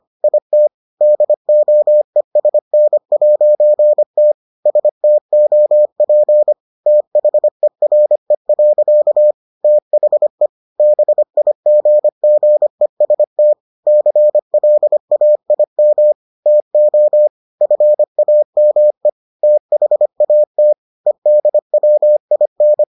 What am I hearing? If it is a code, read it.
Morse code, heard as a keyed tone, about 25 wpm, 600 Hz. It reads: IT DOESN'T STOP THERE. THE BIGGEST CLAIM TO FAME THAT EDWIN